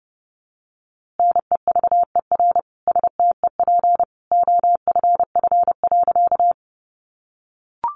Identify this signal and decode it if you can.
Morse code, heard as a keyed tone, about 30 wpm, 700 Hz. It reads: NEVER STEP OFF.
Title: A Sound Of Thunder 30wpm